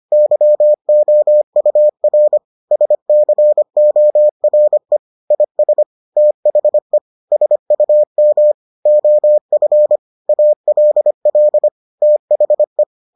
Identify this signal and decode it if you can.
Morse code, heard as a keyed tone, about 25 wpm, 600 Hz. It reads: YOUR SCORE IS THE SUM OF ALL THE